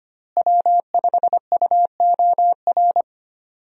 Morse code, heard as a keyed tone, about 25 wpm, 700 Hz. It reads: W5UOR
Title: USA Call Signs - 5 Characters - Repeated 3x - 25wpm